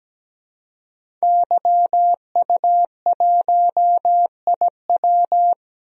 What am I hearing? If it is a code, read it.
Morse code, heard as a keyed tone, about 17 wpm, 700 Hz. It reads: YU1IW
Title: DXCC-100 Call Signs - 5 Characters - 17wpm